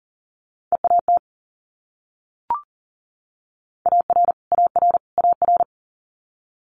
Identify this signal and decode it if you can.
Morse code, heard as a keyed tone, about 40 wpm, 700 Hz. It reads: EAT AR AR AR
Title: Multi-letter Phonemes Repeated 3x 40wpm